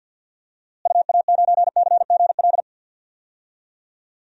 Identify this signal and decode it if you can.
Morse code, heard as a keyed tone, about 50 wpm, 700 Hz. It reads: UA9CGL
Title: DXCC-100 Call Signs - 6 Characters - Repeated 3x - 50wpm